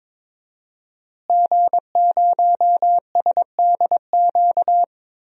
Morse code, heard as a keyed tone, about 22 wpm, 700 Hz. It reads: G0SDQ
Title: DXCC-100 Call Signs - 5 Characters - Repeated 3x - 22wpm